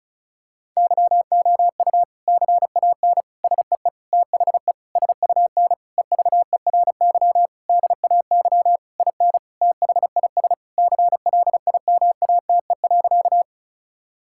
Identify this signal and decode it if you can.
Morse code, heard as a keyed tone, about 34 wpm, 700 Hz. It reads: YOU CAN SEE THE SUN EVERY DAY IN THIS CLIMATE.
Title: Sentences from Top 1600 Words 35wpm